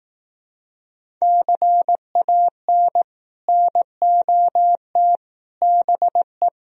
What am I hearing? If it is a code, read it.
Morse code, heard as a keyed tone, about 18 wpm, 700 Hz. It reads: CAN NOT BE
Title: Speed-Racing - Sets of 5 Words 15wpm